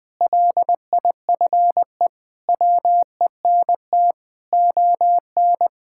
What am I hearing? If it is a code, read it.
Morse code, heard as a keyed tone, about 20 wpm, 700 Hz. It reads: LIFE WENT ON
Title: Sentences with 5-Word Phrases - 20wpm